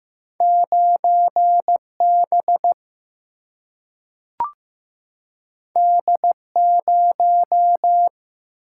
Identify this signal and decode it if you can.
Morse code, heard as a keyed tone, about 15 wpm, 700 Hz. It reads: 9B D0
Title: Single-Letter Single-Number 15wpm